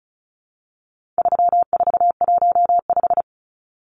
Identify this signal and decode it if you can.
Morse code, heard as a keyed tone, about 34 wpm, 700 Hz. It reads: 3415
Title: Numbers - 4-Digit 35wpm